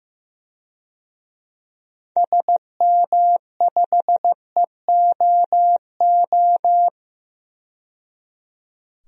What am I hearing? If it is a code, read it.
Morse code, heard as a keyed tone, about 15 wpm, 700 Hz. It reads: SM5EOO
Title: DXCC-100 Call Signs - 6 Characters - 15wpm